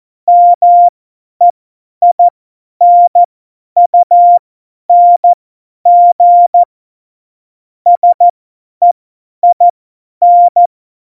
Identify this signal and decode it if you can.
Morse code, heard as a keyed tone, about 14 wpm, spaced as for 10 wpm, 700 Hz. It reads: MEINUNG SEIN
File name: Morsekurs / Witze / Witz 24.Dezember 2019, 50 F70